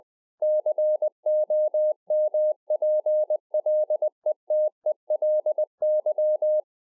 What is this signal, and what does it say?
Morse code, heard as a keyed tone, about 20 wpm, 610 Hz. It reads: COMPLETELY